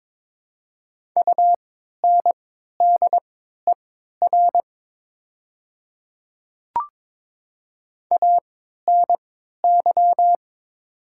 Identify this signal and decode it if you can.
Morse code, heard as a keyed tone, about 22 wpm, spaced as for 12 wpm, 700 Hz. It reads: UNDER ANY
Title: 3x Character Spacing - Top 200 Words - 22wpm